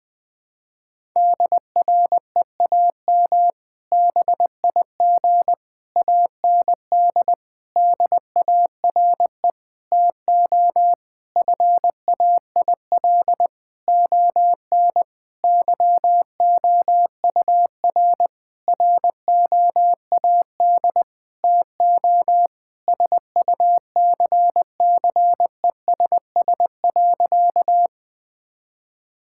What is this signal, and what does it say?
Morse code, heard as a keyed tone, about 21 wpm, 700 Hz. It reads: DREAM BIG AND DARE TO FAIL ON YOUR ROAD TO SUCCESS.
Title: Difficult Sentences with Hints - 20wpm